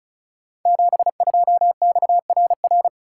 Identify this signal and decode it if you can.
Morse code, heard as a keyed tone, about 36 wpm, 700 Hz. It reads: 72XRR
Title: Code Groups - 5 Letter:Numbers - 35wpm